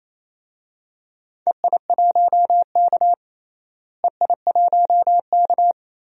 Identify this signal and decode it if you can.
Morse code, heard as a keyed tone, about 28 wpm, 700 Hz. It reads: EI1K EI1K
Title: DXCC-100 Call Signs - 4 Characters - Repeated 2x - 28wpm